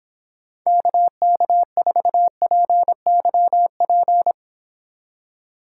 Morse code, heard as a keyed tone, about 26 wpm, 700 Hz. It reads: KK4PYP